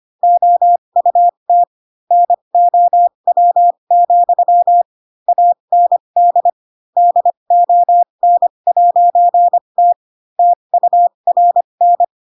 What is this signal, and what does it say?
Morse code, heard as a keyed tone, about 25 wpm, spaced as for 22 wpm, 700 Hz. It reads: OUT NOW, AND DON'T TURN
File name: Pinocchio / Pinocchio ch4 18 25 ik1hge